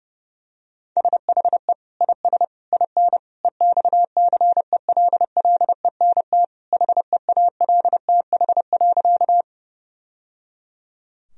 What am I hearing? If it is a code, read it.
Morse code, heard as a keyed tone, about 30 wpm, 700 Hz. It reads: SHE IS IN EXCELLENT HEALTH.
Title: Sentences from Top 1000 Words - Review Part 3 - 30wpm